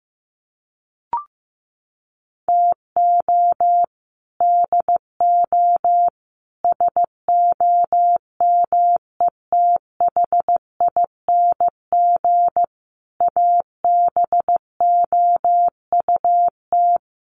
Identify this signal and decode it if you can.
Morse code, heard as a keyed tone, about 15 wpm, 700 Hz. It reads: TO DO SOMETHING ABOUT
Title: Sets of 5 Words 15wpm